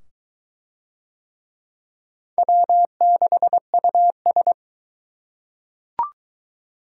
Morse code, heard as a keyed tone, about 22 wpm, 700 Hz. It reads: W6US